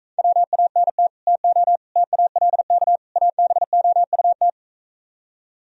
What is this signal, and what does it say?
Morse code, heard as a keyed tone, about 45 wpm, 700 Hz. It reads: WANT TO TALK ABOUT